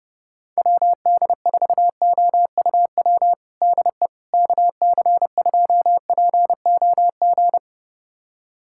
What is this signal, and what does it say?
Morse code, heard as a keyed tone, about 30 wpm, 700 Hz. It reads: WD4OUW DE KC2POG
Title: USA Callsign DE Callsign 30wpm